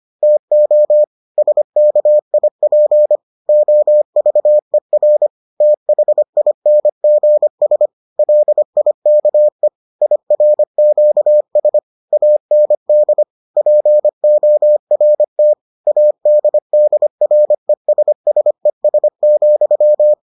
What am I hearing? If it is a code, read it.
Morse code, heard as a keyed tone, about 25 wpm, 600 Hz. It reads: TO SKIP OVER THINGS LIKE IRQS AND PORT ADDRESSES,